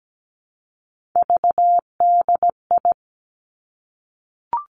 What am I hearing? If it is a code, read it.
Morse code, heard as a keyed tone, about 17 wpm, 700 Hz. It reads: VDI
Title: Three Challenging Letter-Number Combinations 17wpm